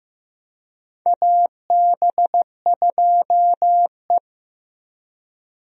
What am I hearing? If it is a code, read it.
Morse code, heard as a keyed tone, about 15 wpm, 700 Hz. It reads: AB2E
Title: USA Call Signs - 4 Characters - Repeated 2x - 15wpm